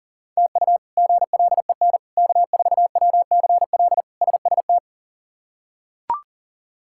Morse code, heard as a keyed tone, about 40 wpm, 700 Hz. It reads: TU GLEN K4WCL SST
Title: K1USN SST QSO Practice - 40wpm